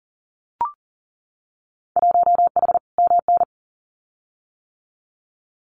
Morse code, heard as a keyed tone, about 40 wpm, 700 Hz. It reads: 1H MN
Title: Winter Field Day Exchanges 40wpm